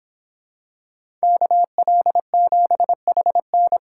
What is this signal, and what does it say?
Morse code, heard as a keyed tone, about 25 wpm, 700 Hz. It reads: KL7HN